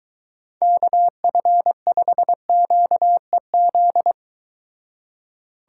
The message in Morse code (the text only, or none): KF5QEZ